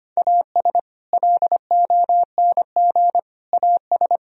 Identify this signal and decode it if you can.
Morse code, heard as a keyed tone, about 32 wpm, 700 Hz. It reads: AS LONG AS